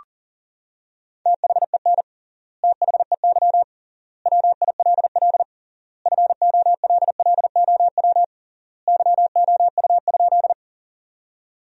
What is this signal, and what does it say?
Morse code, heard as a keyed tone, about 40 wpm, 700 Hz. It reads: THEN THEY WILL FOLLOW YOU?
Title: 3x Word-Spacing - Sentences from Top 200 Words - 40wpm